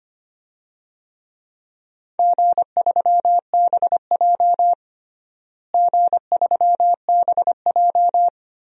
Morse code, heard as a keyed tone, about 25 wpm, 700 Hz. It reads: G3BJ G3BJ
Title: DXCC-100 Call Signs - 4 Characters - Repeated 2x - 25wpm